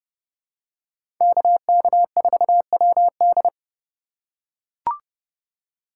Morse code, heard as a keyed tone, about 30 wpm, 700 Hz. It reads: KK4WD
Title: USA Call Signs - 5 Characters - 2x Character Spacing - 30wpm